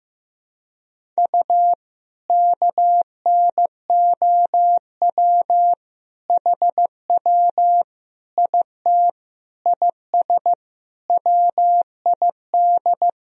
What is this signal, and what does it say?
Morse code, heard as a keyed tone, about 15 wpm, 700 Hz. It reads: U KNOW HW IT IS WID
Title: Sentences from Top 100 Words and Abbreviations 15wpm